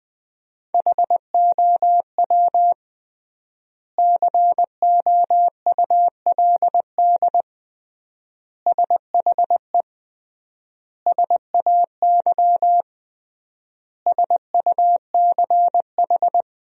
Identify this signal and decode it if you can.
Morse code, heard as a keyed tone, about 21 wpm, 700 Hz. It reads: HOW COULD SHE SAY SUCH
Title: 3x Word-Spacing - Sentences from Top 200 Words - 20wpm